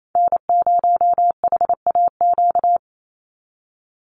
Morse code, heard as a keyed tone, about 28 wpm, 700 Hz. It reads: N0HAQ
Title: USA Call Signs 28wpm